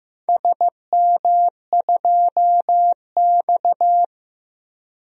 Morse code, heard as a keyed tone, about 16 wpm, 700 Hz. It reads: SM2X
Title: DXCC-100 Call Signs - 4 Characters - 15wpm